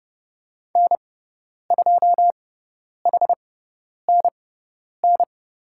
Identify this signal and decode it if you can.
Morse code, heard as a keyed tone, about 30 wpm, spaced as for 10 wpm, 700 Hz. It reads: N2HNN
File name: Lesson 18g - Letter m - Call Signs 30wpm